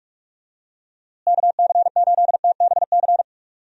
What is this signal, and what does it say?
Morse code, heard as a keyed tone, about 46 wpm, 700 Hz. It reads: KC8TBC
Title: USA Call Signs - 6 Characters - Repeated 2x - 45wpm